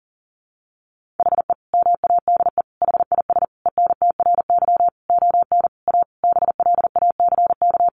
Morse code, heard as a keyed tone, about 40 wpm, 700 Hz. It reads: HE MADE HIS ENTRY ON A BLACK
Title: Sentences from Top 1500 Words 40wpm